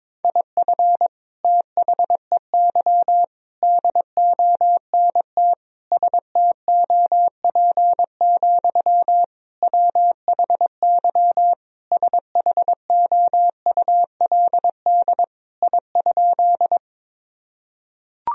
Morse code, heard as a keyed tone, about 22 wpm, 700 Hz. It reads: IF THEY DONT STOP, WHY SHOULD I?
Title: Sentences from Top 1000 Words - Review Part 1 - 22wpm